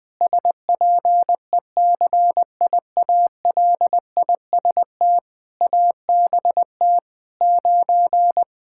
Morse code, heard as a keyed tone, about 21 wpm, 700 Hz. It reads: SPECIALIST ABT 9
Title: Practice QSO - Exchange 3 - More Info - 20wpm